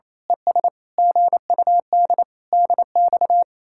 Morse code, heard as a keyed tone, about 26 wpm, 700 Hz. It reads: ES GUD DX
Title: Speed-Racing Common QSO Phrases Encore - 28wpm